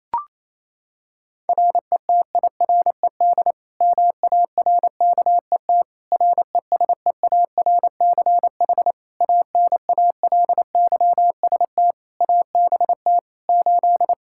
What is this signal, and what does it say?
Morse code, heard as a keyed tone, about 27 wpm, 700 Hz. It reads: RETIRED MARKET RESEARCH ANALYST ABT 8
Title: Practice QSO - Exchange 3 - More Info - 28wpm